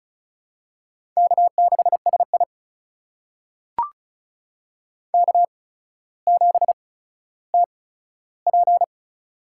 Morse code, heard as a keyed tone, about 36 wpm, spaced as for 15 wpm, 700 Hz. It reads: K6SI K7TP
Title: USA Call Signs - 4 Characters - 8x Character Spacing - 35wpm